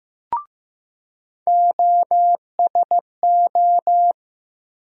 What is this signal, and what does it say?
Morse code, heard as a keyed tone, about 15 wpm, 700 Hz. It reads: OSO
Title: USA Call Sign Postfixes - 3 Letters 15wpm